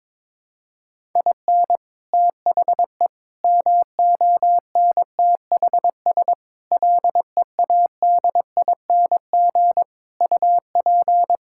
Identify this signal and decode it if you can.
Morse code, heard as a keyed tone, about 22 wpm, 700 Hz. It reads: IN THE MONTHS LEADING UP
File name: practice / 22wpm / Sets of 6 Words - 22wpm